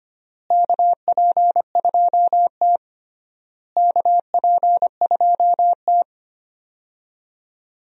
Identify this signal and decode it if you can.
Morse code, heard as a keyed tone, about 25 wpm, 700 Hz. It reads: KP2T KP2T
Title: DXCC-100 Call Signs - 4 Characters - Repeated 3x - 25wpm